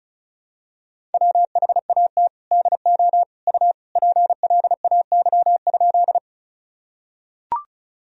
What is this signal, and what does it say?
Morse code, heard as a keyed tone, about 36 wpm, 700 Hz. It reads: WHAT DO U PLAY?
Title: Sentences from Top 200 Words and Abbreviations 35wpm